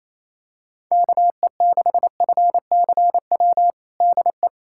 Morse code, heard as a keyed tone, about 28 wpm, 700 Hz. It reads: KE6FCW DE
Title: USA Callsign DE Callsign 28wpm